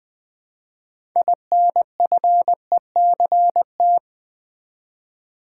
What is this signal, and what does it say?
Morse code, heard as a keyed tone, about 20 wpm, 700 Hz. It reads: INFECT
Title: 6 Letter Words - First 3 Letters Given - 20wpm